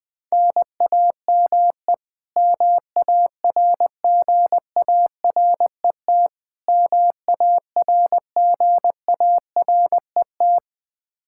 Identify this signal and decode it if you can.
Morse code, heard as a keyed tone, about 20 wpm, 700 Hz. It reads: NAME MARGARET MARGARET
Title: K1USN SST QSO Practice 20wpm